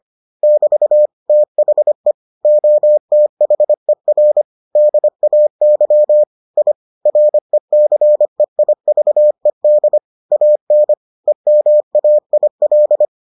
Morse code, heard as a keyed tone, about 25 wpm, 600 Hz. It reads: <BT> THE OTHER DAY I RECEIVED AN EMAIL